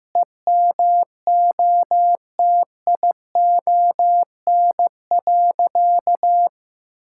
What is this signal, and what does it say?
Morse code, heard as a keyed tone, about 16 wpm, 700 Hz. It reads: EMOTION.
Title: Sentences from Top 1600 Words 15wpm